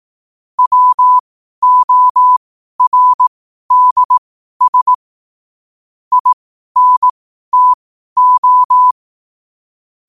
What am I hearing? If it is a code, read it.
Morse code, heard as a keyed tone, about 18 wpm, spaced as for 12 wpm, 1000 Hz. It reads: WORDS INTO
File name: qotdfiles / 12wpm